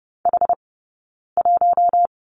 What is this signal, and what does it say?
Morse code, heard as a keyed tone, about 30 wpm, spaced as for 9 wpm, 690 Hz. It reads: H1
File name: Lesson 19d - Letter w - Two Characters 30wpm